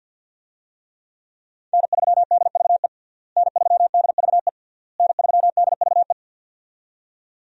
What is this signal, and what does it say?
Morse code, heard as a keyed tone, about 52 wpm, 700 Hz. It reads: N3DVE N3DVE N3DVE
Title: USA Call Signs - 5 Characters - Repeated 3x - 50wpm